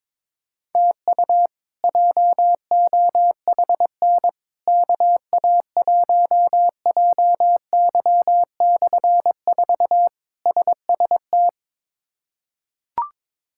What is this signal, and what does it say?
Morse code, heard as a keyed tone, about 22 wpm, 700 Hz. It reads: TU JOHN KA1JY/4 SST
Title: K1USN SST QSO Practice - 22wpm